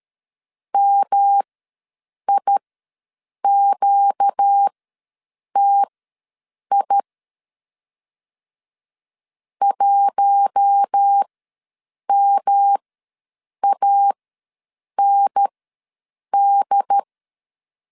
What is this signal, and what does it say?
Morse code, heard as a keyed tone, about 13 wpm, spaced as for 7 wpm, 800 Hz. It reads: MIQTI 1MAND
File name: LEZIONI / 5 LBVC / 5 L ws 12wpm 2958